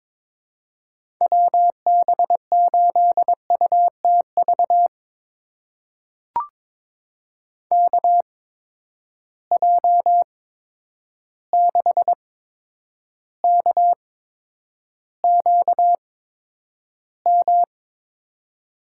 Morse code, heard as a keyed tone, about 22 wpm, spaced as for 9 wpm, 700 Hz. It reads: WB8UTV KJ6KQM
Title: USA Call Signs - 6 Characters - 8x Character Spacing - 22wpm